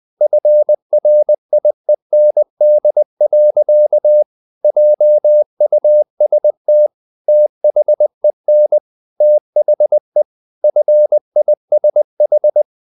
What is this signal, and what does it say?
Morse code, heard as a keyed tone, about 20 wpm, 600 Hz. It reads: FRIEND. JUST THEN THE FISH
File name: the old man and the sea / the old man and the sea part 1 20wpm